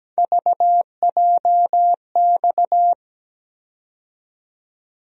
Morse code, heard as a keyed tone, about 17 wpm, 700 Hz. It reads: VJX
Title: USA Call Sign Postfixes - 3 Letters 17wpm